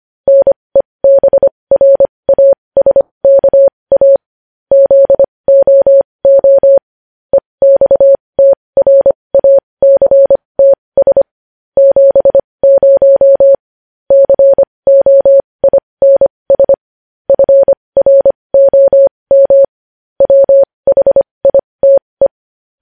Morse code, heard as a keyed tone, about 25 wpm, spaced as for 20 wpm, 550 Hz. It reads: NEBRASKA ZOO EXTRACTS 70 COINS FROM WHITE
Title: News Headlines at 20